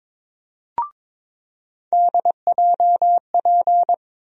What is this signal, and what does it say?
Morse code, heard as a keyed tone, about 23 wpm, 700 Hz. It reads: DJP
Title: USA Call Sign Postfixes - 3 Letters 22wpm